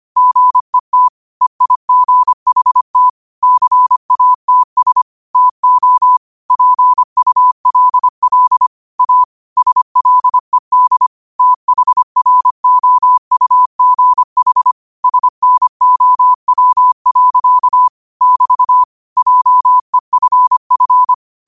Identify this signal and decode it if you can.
Morse code, heard as a keyed tone, about 25 wpm, 1000 Hz. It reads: GET EIGHT CATS TO PULL A SLED THROUGH SNOW. <BT> JEFF